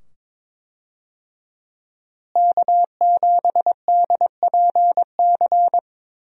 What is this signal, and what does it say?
Morse code, heard as a keyed tone, about 22 wpm, 700 Hz. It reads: K7DPC
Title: USA Call Signs - 5 Characters - Repeated 2x - 22wpm